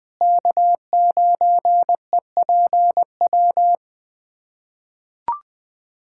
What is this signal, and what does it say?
Morse code, heard as a keyed tone, about 20 wpm, 700 Hz. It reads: K9EPW
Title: USA Callsign DE Callsign - 5 Characters - 20wpm